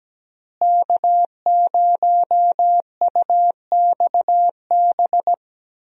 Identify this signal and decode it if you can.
Morse code, heard as a keyed tone, about 17 wpm, 700 Hz. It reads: K0UXB
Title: USA Call Signs - 5 Characters - Repeated 2x - 17wpm